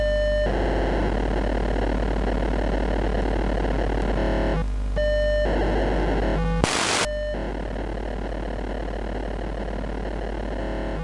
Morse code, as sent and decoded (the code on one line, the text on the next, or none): none